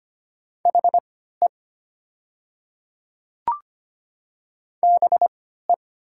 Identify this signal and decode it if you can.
Morse code, heard as a keyed tone, about 25 wpm, spaced as for 14 wpm, 700 Hz. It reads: HE BE
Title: 3x Character Spacing - Top 100 Words - 25wpm